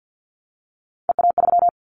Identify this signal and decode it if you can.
Morse code, heard as a keyed tone, about 50 wpm, 700 Hz. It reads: EA3
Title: Callsign Prefixes for Top 25 DXCC - Repeated 3x - 50wpm